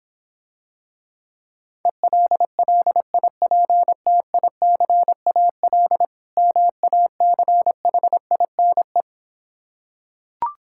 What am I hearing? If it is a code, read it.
Morse code, heard as a keyed tone, about 26 wpm, 700 Hz. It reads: ELLIPTICAL MACHINE